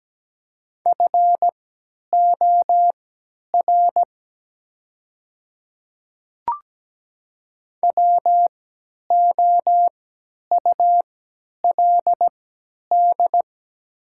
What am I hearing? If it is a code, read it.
Morse code, heard as a keyed tone, about 17 wpm, spaced as for 10 wpm, 700 Hz. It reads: FOR WOULD
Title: 3x Character Spacing - Top 100 Words - 17wpm